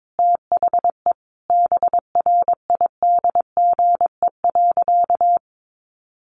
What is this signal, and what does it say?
Morse code, heard as a keyed tone, about 22 wpm, 700 Hz. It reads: THE BRIDGE.
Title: Sentences from Top 1200 Words 22wpm